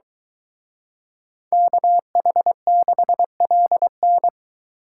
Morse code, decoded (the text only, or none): KH6LN